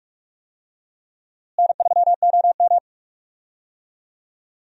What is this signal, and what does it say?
Morse code, heard as a keyed tone, about 46 wpm, 700 Hz. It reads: N3OM
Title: USA Call Signs - 4 Characters - Repeated 3x - 45wpm